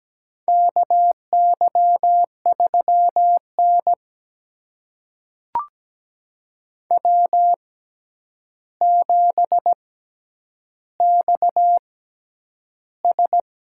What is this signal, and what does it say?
Morse code, heard as a keyed tone, about 17 wpm, spaced as for 9 wpm, 700 Hz. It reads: KY3N W7XS
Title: USA Call Signs - 4 Characters - 6x Character Spacing - 17wpm